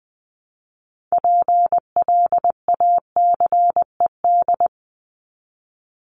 Morse code, heard as a keyed tone, about 20 wpm, 700 Hz. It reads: PLACED
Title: ED Suffix - Top 100 Words with Base Words 20wpm